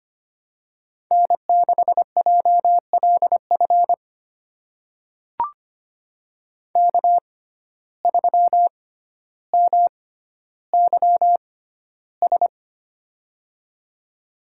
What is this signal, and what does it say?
Morse code, heard as a keyed tone, about 25 wpm, spaced as for 13 wpm, 700 Hz. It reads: N6JLF K3MYS
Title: USA Call Signs - 5 Characters - 6x Character Spacing - 25wpm